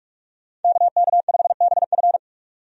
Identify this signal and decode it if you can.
Morse code, heard as a keyed tone, about 44 wpm, 700 Hz. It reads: KK5BF
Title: USA Call Signs - 5 Characters - 45wpm